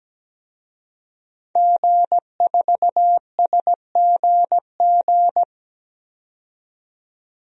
Morse code, heard as a keyed tone, about 17 wpm, 700 Hz. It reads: G4SGG
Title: DXCC-100 Call Signs - 5 Characters - 17wpm